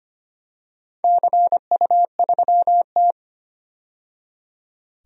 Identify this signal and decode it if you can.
Morse code, heard as a keyed tone, about 25 wpm, 700 Hz. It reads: CU3T
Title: DXCC-100 Call Signs - 4 Characters - 25wpm